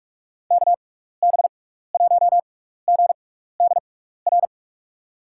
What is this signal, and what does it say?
Morse code, heard as a keyed tone, about 43 wpm, spaced as for 15 wpm, 700 Hz. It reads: KB1GDR
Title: USA Call Signs - 6x Character Spacing - 45wpm